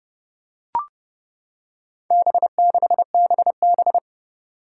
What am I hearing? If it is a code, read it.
Morse code, heard as a keyed tone, about 30 wpm, 700 Hz. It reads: B6BB
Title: Four Challenging Letter-Number Combinations 30wpm